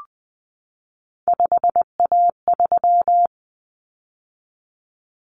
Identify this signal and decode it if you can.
Morse code, heard as a keyed tone, about 20 wpm, 700 Hz. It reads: 5A3
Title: Common Cut Numbers - 3-Digit 20wpm